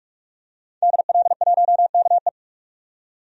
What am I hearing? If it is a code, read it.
Morse code, heard as a keyed tone, about 46 wpm, 700 Hz. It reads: DL1KE
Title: DXCC-100 Call Signs - 5 Characters - 45wpm